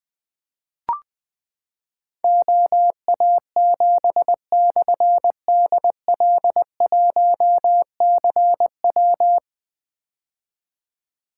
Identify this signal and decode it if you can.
Morse code, heard as a keyed tone, about 21 wpm, 700 Hz. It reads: OA7/DL1CW
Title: Difficult Call Signs 20wpm